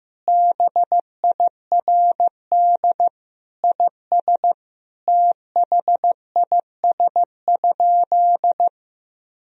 Morse code, heard as a keyed tone, about 15 wpm, 700 Hz. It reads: BIRD IS THIS?
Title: Sentences from Top 1000 Words - Review Part 2 - 15wpm